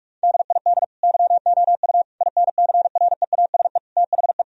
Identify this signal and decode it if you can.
Morse code, heard as a keyed tone, about 46 wpm, 700 Hz. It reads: DID YOU INCREASE THE
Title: Sentences from Top 1000 Words - Review Part 4 - 45wpm